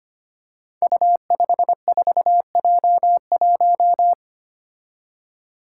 Morse code, heard as a keyed tone, about 25 wpm, 700 Hz. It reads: U54J1
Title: Five Challenging Letter-Number Combinations 25wpm